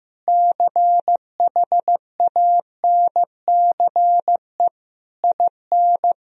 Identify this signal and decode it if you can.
Morse code, heard as a keyed tone, about 15 wpm, 700 Hz. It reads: CHANCE IN
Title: Sentences from Top 1000 Words - Review Part 3 - 15wpm